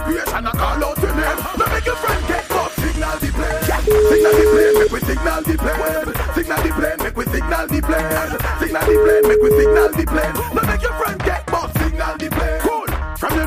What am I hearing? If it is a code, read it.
Ringback tone (Europe, 425 Hz), with noise.